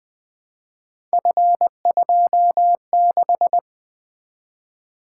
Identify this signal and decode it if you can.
Morse code, heard as a keyed tone, about 20 wpm, 700 Hz. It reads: F26
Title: Three Letter-Number Combinations - 20wpm